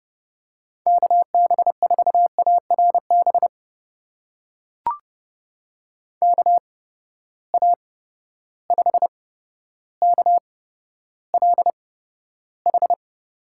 Morse code, heard as a keyed tone, about 30 wpm, spaced as for 13 wpm, 700 Hz. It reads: KB4ARB KA5KLH
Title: USA Call Signs - 6 Characters - 8x Character Spacing - 30wpm